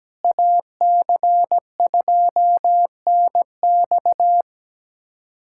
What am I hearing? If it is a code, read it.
Morse code, heard as a keyed tone, about 17 wpm, 700 Hz. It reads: AC2NX
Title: USA Callsign DE Callsign - 5 Characters - 17wpm